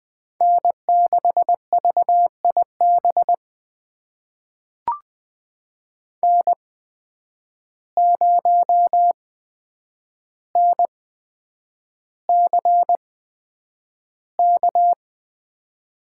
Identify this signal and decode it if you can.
Morse code, heard as a keyed tone, about 20 wpm, spaced as for 9 wpm, 700 Hz. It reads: N6VIB N0NCK
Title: USA Call Signs - 5 Characters - 8x Character Spacing - 20wpm